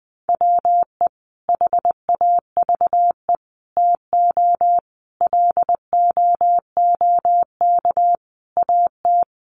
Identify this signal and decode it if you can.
Morse code, heard as a keyed tone, about 20 wpm, 700 Hz. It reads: WE HAVE TO LOOK AT
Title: Sets of 5 Words Encore2 20wpm